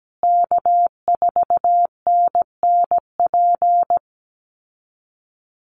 Morse code, heard as a keyed tone, about 17 wpm, 700 Hz. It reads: K4NNP
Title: USA Call Signs Repeated 3x 17wpm